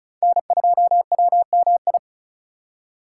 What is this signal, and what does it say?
Morse code, heard as a keyed tone, about 34 wpm, 700 Hz. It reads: N2WMI
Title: USA Call Signs - 5 Characters - 3x Character Spacing - 35wpm